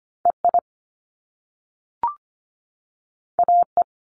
Morse code, heard as a keyed tone, about 25 wpm, 700 Hz. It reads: EI AE
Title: Top Two Letter Combinations in Words 25wpm